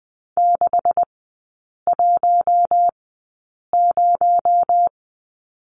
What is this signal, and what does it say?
Morse code, heard as a keyed tone, about 20 wpm, spaced as for 8 wpm, 700 Hz. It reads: 610
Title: 218 Lesson 35e - Number 0 - Three Characters 20wpm